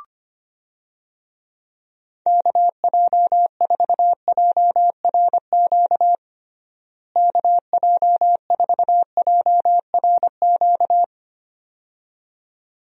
Morse code, heard as a keyed tone, about 25 wpm, 700 Hz. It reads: KJ4JRQ KJ4JRQ
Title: USA Call Signs - 6 Characters - Repeated 2x - 25wpm